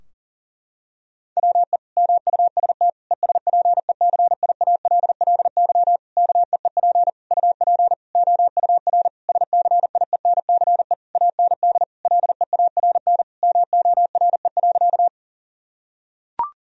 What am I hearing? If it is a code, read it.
Morse code, heard as a keyed tone, about 37 wpm, 700 Hz. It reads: WE MUST ESPECIALLY KEEP UP OUR SCIENCE AND LEARN MORE.
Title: Sentences from Top 1000 Words - Review Part 4 - 40wpm